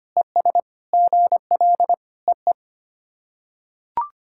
Morse code, heard as a keyed tone, about 25 wpm, 690 Hz. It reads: ES GL EE
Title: SOTA QSO Practice 25wpm